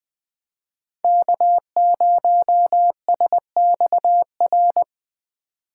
Morse code, heard as a keyed tone, about 20 wpm, 700 Hz. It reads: K0SXR
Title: USA Call Signs - 5 Characters - 2x Character Spacing - 20wpm